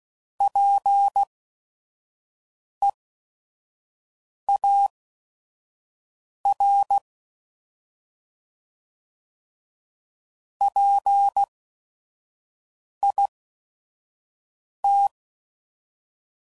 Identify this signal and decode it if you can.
Morse code, heard as a keyed tone, about 16 wpm, spaced as for 5 wpm, 800 Hz. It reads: PEAR PIT
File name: Prefixes / By First Letter / Ps